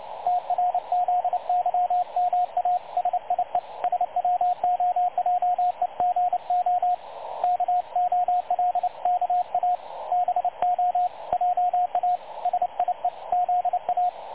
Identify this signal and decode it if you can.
Morse code, heard as a keyed tone, about 29 wpm, 720 Hz, with radio noise. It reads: TRZYMA SIE SWOJEGO KOLKA BO JA SIE ZA
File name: dlazaawansowanych / nagraniazpasma / websdr recording 3565.0kHz